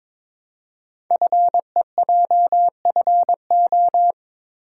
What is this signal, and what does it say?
Morse code, heard as a keyed tone, about 23 wpm, 700 Hz. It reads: FEJFO
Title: Code Groups - 5 Letters - 22wpm